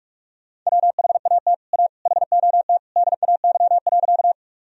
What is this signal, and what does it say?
Morse code, heard as a keyed tone, about 46 wpm, 700 Hz. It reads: WHAT A HOT DAY.
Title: Sentences from Top 1000 Words - Review Part 3 - 45wpm